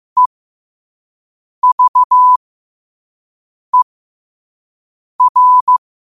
Morse code, heard as a keyed tone, about 15 wpm, spaced as for 5 wpm, 1000 Hz. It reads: EVER
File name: qotdfiles / 5wpm